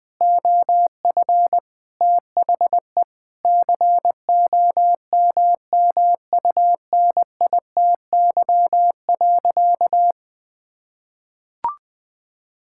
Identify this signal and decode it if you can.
Morse code, heard as a keyed tone, about 20 wpm, 700 Hz. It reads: OF THE COMMUNITY.
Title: Sentences with 8-Word Phrases - 20wpm